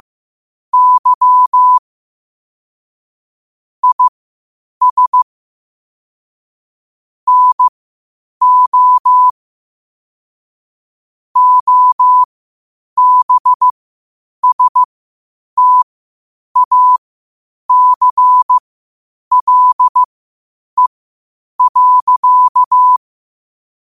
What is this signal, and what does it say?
Morse code, heard as a keyed tone, about 15 wpm, spaced as for 8 wpm, 1000 Hz. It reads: Y IS NO OBSTACLE.
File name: qotdfiles / 8wpm